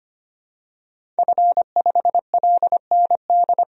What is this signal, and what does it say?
Morse code, heard as a keyed tone, about 25 wpm, 700 Hz. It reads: F5LND
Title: DXCC-100 Call Signs - 5 Characters - 25wpm